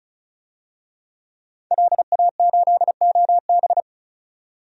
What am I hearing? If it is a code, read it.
Morse code, heard as a keyed tone, about 34 wpm, 700 Hz. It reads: LA8OB
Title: DXCC-100 Call Signs - 5 Characters - 35wpm